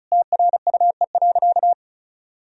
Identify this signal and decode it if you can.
Morse code, heard as a keyed tone, about 34 wpm, 700 Hz. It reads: TRUE.
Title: Sentences from Top 1000 Words - Review Part 4 - 35wpm